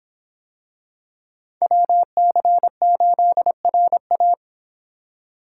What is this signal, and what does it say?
Morse code, heard as a keyed tone, about 26 wpm, 700 Hz. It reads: WC8RA